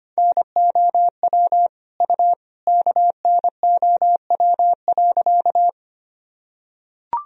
Morse code, heard as a keyed tone, about 25 wpm, 700 Hz. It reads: NOW U KNOW.
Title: Sentences from Top 100 Words and Abbreviations 25wpm